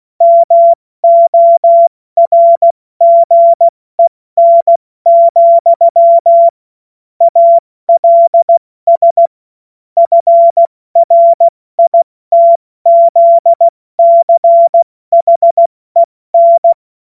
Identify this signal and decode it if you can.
Morse code, heard as a keyed tone, about 16 wpm, 680 Hz. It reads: MORGEN, ALS FRITZCHEN
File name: Morsekurs / Witze / Witz 28. September 2021, 70 F80